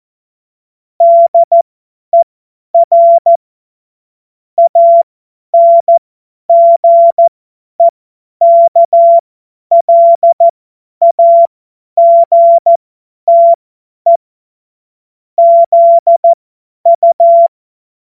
Morse code, heard as a keyed tone, about 14 wpm, spaced as for 10 wpm, 680 Hz. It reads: DER ANGEKLAGTE ZU
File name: Morsekurs / Witze / Witz 03. Mai 2022 50 F70